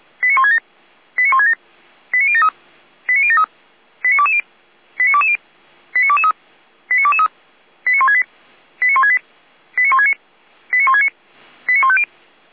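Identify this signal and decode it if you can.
Five-tone selective call (ZVEI), heard as quick runs of five tones, tones of about 80 ms. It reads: 89167 89167 89073 89073 89220 89220 89223 89223 89168 89168 89169 89169 89150
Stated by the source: ZVEI five-tone call 89022, received by SDR for Feuerwehr Orlamünde.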